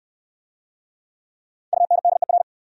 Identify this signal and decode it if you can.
Morse code, heard as a keyed tone, about 65 wpm, 700 Hz. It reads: UNDER